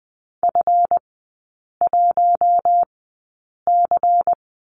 Morse code, heard as a keyed tone, about 20 wpm, spaced as for 8 wpm, 700 Hz. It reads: F1C
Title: 138 Lesson 23e - Letter f - Three Characters 20wpm